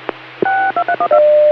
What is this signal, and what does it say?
DTMF, heard as telephone touch tones, tones of about 80 ms. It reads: 6231